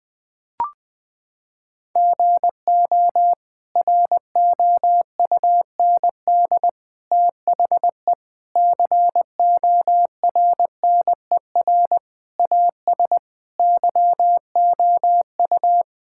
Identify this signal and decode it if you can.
Morse code, heard as a keyed tone, about 21 wpm, 700 Hz. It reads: GO ROUND THE CORNER AS YOU
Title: Sentences from Top 1600 Words 20wpm